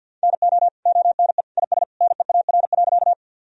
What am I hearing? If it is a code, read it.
Morse code, heard as a keyed tone, about 52 wpm, 700 Hz. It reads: NO ONE IS NEAR.
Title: Sentences from Top 1000 Words - Review Part 1 - 50wpm